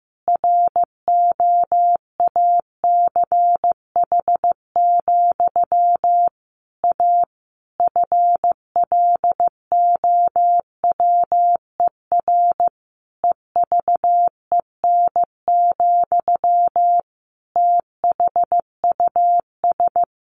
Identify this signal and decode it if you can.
Morse code, heard as a keyed tone, about 15 wpm, 700 Hz. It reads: ROACH, A FLOWER EVEN, THUS
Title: A Sound Of Thunder 15wpm